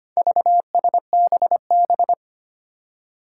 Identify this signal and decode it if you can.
Morse code, heard as a keyed tone, about 25 wpm, 700 Hz. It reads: VSBB
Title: Four Challenging Letter-Number Combinations 25wpm